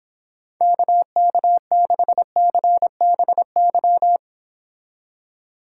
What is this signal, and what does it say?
Morse code, heard as a keyed tone, about 31 wpm, 700 Hz. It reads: KK6CBY